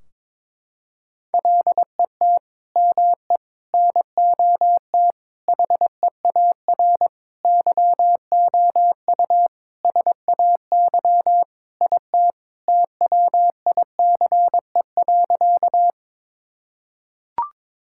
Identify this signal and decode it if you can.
Morse code, heard as a keyed tone, about 22 wpm, 700 Hz. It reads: LET ME NOT HEAR YOU SAY IT TWICE.
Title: Sentences from Top 1600 Words 22wpm